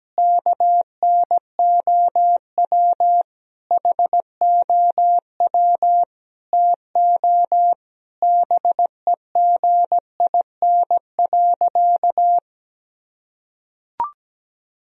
Morse code, heard as a keyed tone, about 17 wpm, 700 Hz. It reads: KNOW HOW TO BEGIN.
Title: Sentences from Top 1000 Words - Review Part 1 - 17wpm